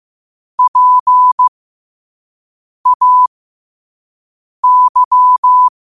Morse code, heard as a keyed tone, about 15 wpm, spaced as for 5 wpm, 1000 Hz. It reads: PAY
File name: qotdfiles / 5wpm